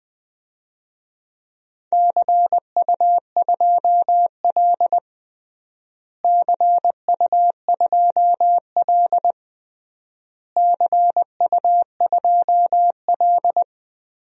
Morse code, heard as a keyed tone, about 20 wpm, 700 Hz. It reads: CU2L CU2L CU2L
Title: DXCC-100 Call Signs - 4 Characters - Repeated 3x - 20wpm